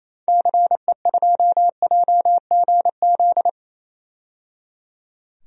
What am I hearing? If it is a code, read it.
Morse code, heard as a keyed tone, about 28 wpm, 700 Hz. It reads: CE2JGZ
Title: DXCC-100 Call Signs - 6 Characters - 28wpm